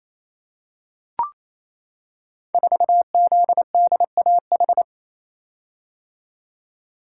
Morse code, heard as a keyed tone, about 27 wpm, 700 Hz. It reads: 4ZDAH
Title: Code Groups - 5 Letter:Numbers - 28wpm